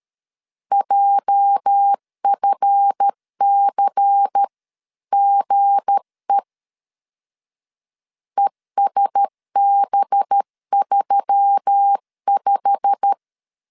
Morse code, heard as a keyed tone, about 13 wpm, 800 Hz. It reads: JFCGE ESB35
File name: LEZIONI / 8 YJ45 / 8 YJ45 12wpm 2151